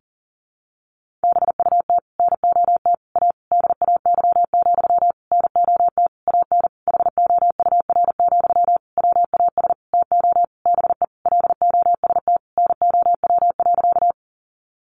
Morse code, heard as a keyed tone, about 40 wpm, 700 Hz. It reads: BUT NOT A DAY, NOT AN HOUR, WAS TO BE LOST NOW.
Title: Sentences from Top 1400 Words 40wpm